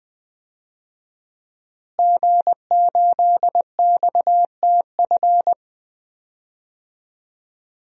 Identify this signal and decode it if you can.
Morse code, heard as a keyed tone, about 20 wpm, 700 Hz. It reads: G8XTF
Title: DXCC-100 Call Signs - 5 Characters - Repeated 3x - 20wpm